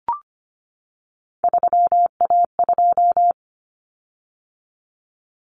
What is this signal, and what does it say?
Morse code, heard as a keyed tone, about 25 wpm, 700 Hz. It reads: 3A2
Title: Common Cut Numbers - 3-Digit 25wpm